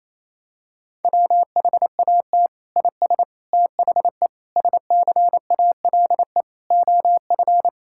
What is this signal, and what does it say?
Morse code, heard as a keyed tone, about 28 wpm, 700 Hz. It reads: WHAT IS THE SCALE OF
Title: Sentences from Top 1000 Words - Review Part 5 - 28wpm